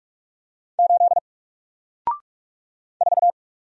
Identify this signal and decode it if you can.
Morse code, heard as a keyed tone, about 46 wpm, 700 Hz. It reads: 8 4
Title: Single Letter-Number Alt 45wpm